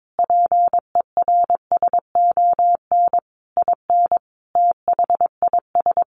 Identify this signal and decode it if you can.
Morse code, heard as a keyed tone, about 22 wpm, 700 Hz. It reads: PERSON IN THIS
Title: Sentences from Top 400 Words 22wpm